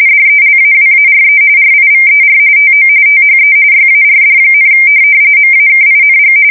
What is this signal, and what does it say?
RTTY, about 45.45 baud. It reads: DELAWARE RIVER TO THE YARD, MAKING
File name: fsk170 / USS-New-Jersey